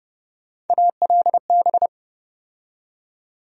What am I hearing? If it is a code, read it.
Morse code, heard as a keyed tone, about 30 wpm, 700 Hz. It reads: ALB
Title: USA Call Sign Postfixes - 3 Letters 30wpm